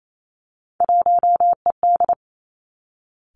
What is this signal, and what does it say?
Morse code, heard as a keyed tone, about 28 wpm, 690 Hz. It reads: 1ED
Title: Three Challenging Letter-Number Combinations 28wpm